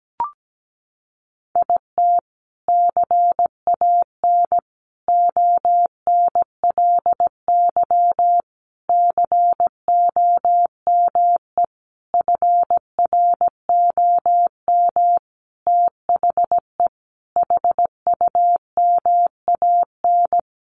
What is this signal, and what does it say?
Morse code, heard as a keyed tone, about 17 wpm, 700 Hz. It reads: IT CAN ONLY COME FROM THE HUMAN
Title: Sentences from Top 1400 Words 17wpm